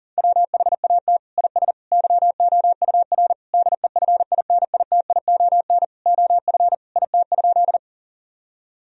Morse code, heard as a keyed tone, about 40 wpm, 700 Hz. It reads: WHAT IS YOUR DEFINITION OF IT?
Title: Sentences from Top 1000 Words - Review Part 4 - 40wpm